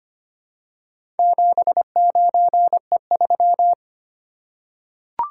Morse code, heard as a keyed tone, about 25 wpm, 700 Hz. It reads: 79E3
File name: practice / 25wpm / Four Letter-Number Combinations - 25wpm